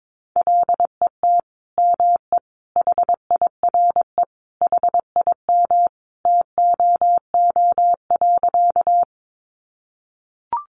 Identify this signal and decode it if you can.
Morse code, heard as a keyed tone, about 22 wpm, 700 Hz. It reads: LET ME HIRE HIM TOO.
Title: Sentences from Top 1400 Words 22wpm